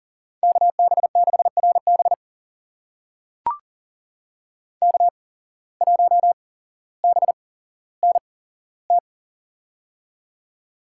Morse code, heard as a keyed tone, about 39 wpm, spaced as for 17 wpm, 700 Hz. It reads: KB6RB K1BNT
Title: USA Call Signs - 5 Characters - 8x Character Spacing - 40wpm